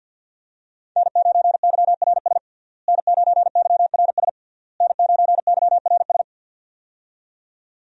Morse code, heard as a keyed tone, about 47 wpm, 700 Hz. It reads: N9YRS N9YRS N9YRS
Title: USA Call Signs - 5 Characters - Repeated 3x - 50wpm